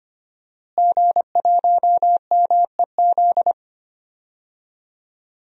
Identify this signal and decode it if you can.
Morse code, heard as a keyed tone, about 25 wpm, 700 Hz. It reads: G1MEZ
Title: DXCC-100 Call Signs - 5 Characters - Repeated 2x - 25wpm